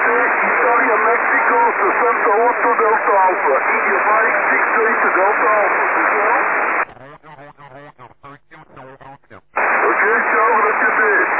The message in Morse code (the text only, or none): none